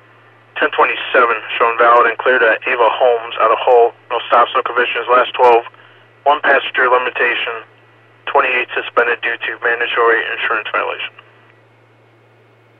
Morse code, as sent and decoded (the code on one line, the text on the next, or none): none